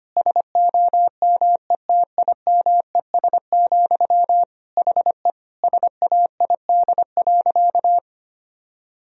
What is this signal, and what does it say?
Morse code, heard as a keyed tone, about 25 wpm, 700 Hz. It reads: SOMETIMES, HE SAID.
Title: Sentences from Top 1000 Words - Review Part 3 - 25wpm